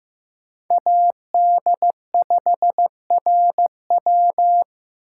Morse code, heard as a keyed tone, about 15 wpm, 700 Hz. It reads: AD5RW
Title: USA Call Signs - 3x Character Spacing - 15wpm